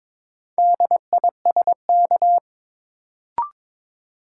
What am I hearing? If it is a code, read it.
Morse code, heard as a keyed tone, about 21 wpm, 700 Hz. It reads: DISK
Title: Sentences from Top 1000 Words - Review Part 6 - New Word Drill - 22wpm